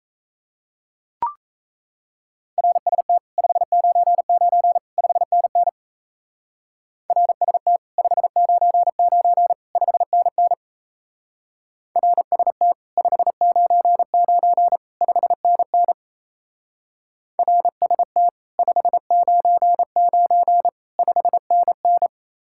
Morse code, heard as a keyed tone, about 35 wpm, 700 Hz. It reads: RST 599 5NN RST 599 5NN RST 599 5NN RST 599 5NN
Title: Speed-Racing Common QSO Phrases - 28wpm